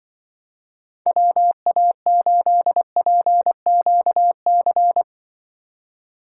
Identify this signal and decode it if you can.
Morse code, heard as a keyed tone, about 24 wpm, 700 Hz. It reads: WA8PQC
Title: Speed Racing - USA 2x3 Call Signs 20wpm